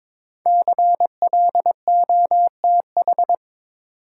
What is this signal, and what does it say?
Morse code, heard as a keyed tone, about 23 wpm, 700 Hz. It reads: CLOTH
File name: 5 Letter Words - First 3 Letters Given - 22wpm